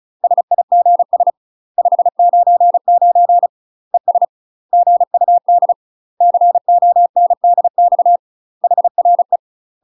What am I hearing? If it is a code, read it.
Morse code, heard as a keyed tone, about 34 wpm, 700 Hz. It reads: SIGS 599 ES GUD CONDX HRE
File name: UK MASTER QSOs TRAINING / 35WPM / UK MASTER QSOs 0015